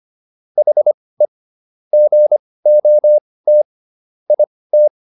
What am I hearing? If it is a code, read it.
Morse code, heard as a keyed tone, about 25 wpm, spaced as for 18 wpm, 600 Hz. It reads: HE GOT IT